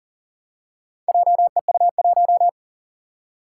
Morse code, heard as a keyed tone, about 40 wpm, 700 Hz. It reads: JEU1
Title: Four Challenging Letter-Number Combinations 40wpm